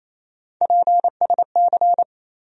Morse code, heard as a keyed tone, about 28 wpm, 700 Hz. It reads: PSC
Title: USA Call Sign Postfixes - 3 Letters 28wpm